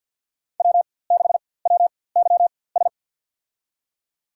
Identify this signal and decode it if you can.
Morse code, heard as a keyed tone, about 52 wpm, spaced as for 24 wpm, 700 Hz. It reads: W6WYS
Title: USA Call Signs - 5 Characters - 4x Character Spacing - 50wpm